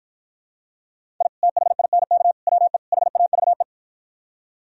Morse code, heard as a keyed tone, about 53 wpm, 700 Hz. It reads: I THINK WE HAVE